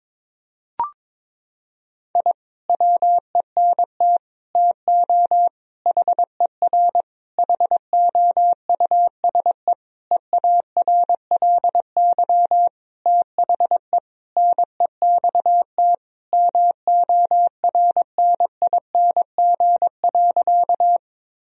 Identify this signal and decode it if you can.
Morse code, heard as a keyed tone, about 22 wpm, 700 Hz. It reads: I WENT TO HER HOUSE EARLY THE NEXT MORNING.
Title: Sentences from Top 1000 Words - Review Part 2 - 22wpm